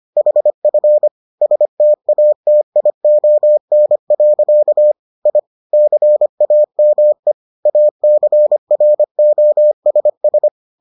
Morse code, heard as a keyed tone, about 25 wpm, 600 Hz. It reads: HF STATION. I CAME ACROSS